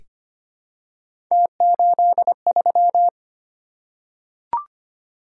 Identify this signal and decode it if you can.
Morse code, heard as a keyed tone, about 25 wpm, 700 Hz. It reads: T83
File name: Three Letter-Number Combinations - 25wpm